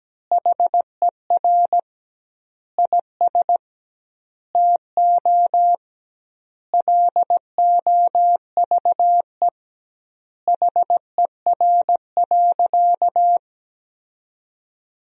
Morse code, heard as a keyed tone, about 17 wpm, 700 Hz. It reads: HER IS TO LOVE HER.
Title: 2x Word-Spacing - Sentences from Top 400 Words - 17wpm